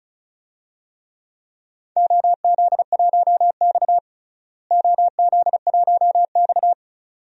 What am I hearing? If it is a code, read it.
Morse code, heard as a keyed tone, about 34 wpm, 700 Hz. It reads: OZ1X OZ1X
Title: DXCC-100 Call Signs - 4 Characters - Repeated 2x - 35wpm